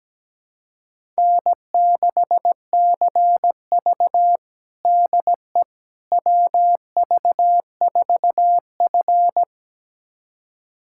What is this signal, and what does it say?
Morse code, heard as a keyed tone, about 17 wpm, 700 Hz. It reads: N6CV DE WV4F
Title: USA Callsign DE Callsign - 4 Characters - 17wpm